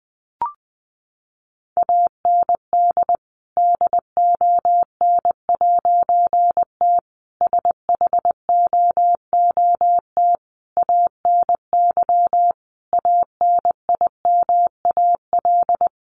Morse code, heard as a keyed tone, about 20 wpm, 700 Hz. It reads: AND DON'T SHOOT ANY ANIMAL
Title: A Sound Of Thunder 20wpm